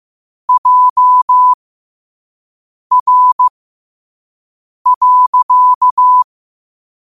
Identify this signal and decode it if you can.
Morse code, heard as a keyed tone, about 15 wpm, spaced as for 5 wpm, 1000 Hz. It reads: JR.
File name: qotdfiles / 5wpm